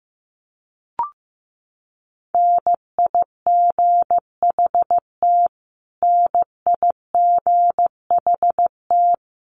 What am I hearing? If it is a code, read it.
Morse code, heard as a keyed tone, about 15 wpm, 700 Hz. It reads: NIGHT NIGHT
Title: Top 500 Words Repeated 15wpm